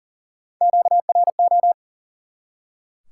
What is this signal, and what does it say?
Morse code, heard as a keyed tone, about 37 wpm, 700 Hz. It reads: QRO
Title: Top 90 QSO Elements 40wpm